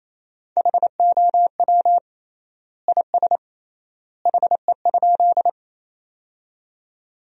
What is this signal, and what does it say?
Morse code, heard as a keyed tone, about 27 wpm, 700 Hz. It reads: HOW IS HE?
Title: 3x Word-Spacing - Sentences from Top 100 Words - 28wpm